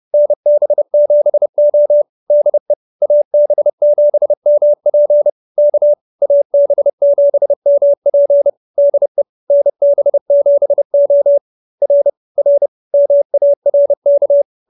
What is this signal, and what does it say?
Morse code, heard as a keyed tone, about 30 wpm, 590 Hz. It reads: NB7O DE AB7MP K AB7MP DE NB7O R R MARK